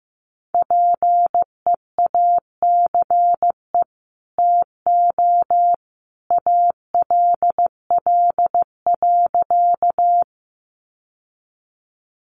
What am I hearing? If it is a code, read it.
Morse code, heard as a keyed tone, about 14 wpm, 700 Hz. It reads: PEACE TO ALL.
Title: Sentences from Top 1500 Words 15wpm